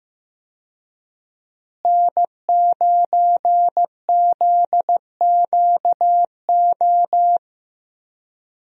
Morse code, heard as a keyed tone, about 15 wpm, 700 Hz. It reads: N9ZQO
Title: USA Call Signs - 5 Characters - Repeated 2x - 15wpm